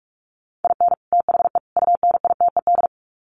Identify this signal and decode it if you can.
Morse code, heard as a keyed tone, about 44 wpm, 700 Hz. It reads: IN THE UNITED
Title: Sets of 3 Words 45wpm